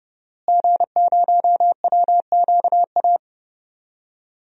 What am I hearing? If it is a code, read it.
Morse code, heard as a keyed tone, about 30 wpm, 700 Hz. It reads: G0WQA
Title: DXCC-100 Call Signs - 5 Characters - Repeated 2x - 30wpm